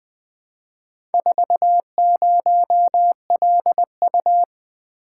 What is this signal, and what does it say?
Morse code, heard as a keyed tone, about 20 wpm, 700 Hz. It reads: 40LU
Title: Four Letter-Number Combinations - 20wpm